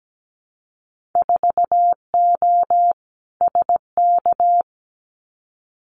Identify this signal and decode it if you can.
Morse code, heard as a keyed tone, about 17 wpm, 700 Hz. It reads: 4O SK
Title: Winter Field Day Exchanges Encore 17wpm